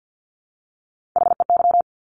Morse code, heard as a keyed tone, about 50 wpm, 700 Hz. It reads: HEY
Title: Top Three Letter Combinations in Words 50wpm